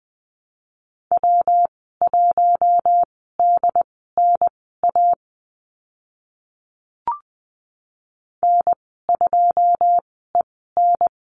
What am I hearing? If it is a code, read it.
Morse code, heard as a keyed tone, about 20 wpm, spaced as for 14 wpm, 700 Hz. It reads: W1DNA N2EN
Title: 122 Lesson 20g - Number 3 - Call Signs 20wpm